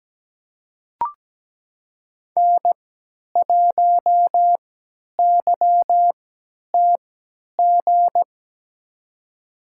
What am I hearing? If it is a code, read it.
Morse code, heard as a keyed tone, about 17 wpm, spaced as for 10 wpm, 700 Hz. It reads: N1YTG
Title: USA Call Signs - 5 Characters - 3x Character Spacing - 17wpm